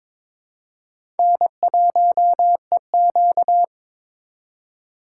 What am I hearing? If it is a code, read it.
Morse code, heard as a keyed tone, about 22 wpm, 700 Hz. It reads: N1EQ
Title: USA Call Signs - 3x Character Spacing - 22wpm